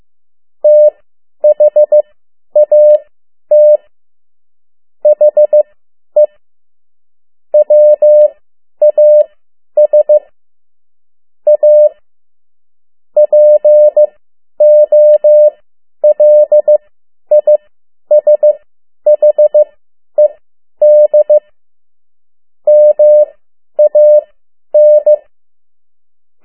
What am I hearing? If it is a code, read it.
Morse code, heard as a keyed tone, about 15 wpm, spaced as for 10 wpm, 600 Hz. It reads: THAT HE WAS A POLISHED MAN
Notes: Morse code 10 WPM